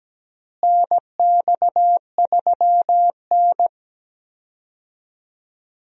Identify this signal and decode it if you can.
Morse code, heard as a keyed tone, about 17 wpm, 700 Hz. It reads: NX3N
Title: USA Call Signs - 4 Characters - Repeated 2x - 17wpm